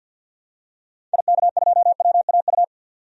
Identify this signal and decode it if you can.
Morse code, heard as a keyed tone, about 51 wpm, 700 Hz. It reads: IK2WAU
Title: DXCC-100 Call Signs - 6 Characters - 50wpm